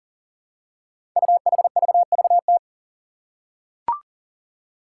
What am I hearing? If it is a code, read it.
Morse code, heard as a keyed tone, about 40 wpm, 700 Hz. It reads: UHVVT
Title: Five Challenging Letter-Number Combinations 40wpm